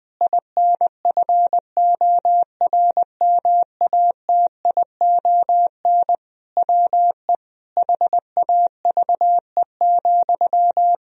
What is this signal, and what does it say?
Morse code, heard as a keyed tone, about 20 wpm, 700 Hz. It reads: INFORMATION WE HAVE,
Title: Sentences from Top 1000 Words - Review Part 4 - 20wpm